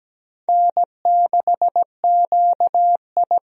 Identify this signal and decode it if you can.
Morse code, heard as a keyed tone, about 17 wpm, 700 Hz. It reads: N6QI
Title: USA Call Signs - 4 Characters - 17wpm